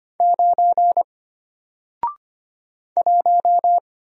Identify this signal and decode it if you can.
Morse code, heard as a keyed tone, about 25 wpm, 700 Hz. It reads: 9 1
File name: Single-Letter-Number-25wpm